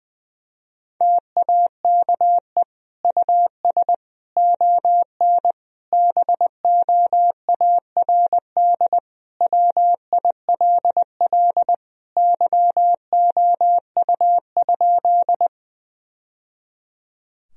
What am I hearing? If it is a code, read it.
Morse code, heard as a keyed tone, about 19 wpm, 700 Hz. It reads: TAKE US ON BOARD WILL YOU?
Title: Sentences from Top 1000 Words - Review Part 4 - 20wpm